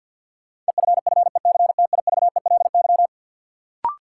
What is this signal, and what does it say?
Morse code, heard as a keyed tone, about 47 wpm, 700 Hz. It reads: EFFECTIVELY
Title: Sentences from Top 1400 Words - New Word Drill Alt - 50wpm